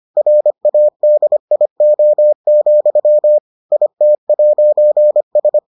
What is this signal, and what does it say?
Morse code, heard as a keyed tone, about 25 wpm, 600 Hz. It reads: RADIO, IT'S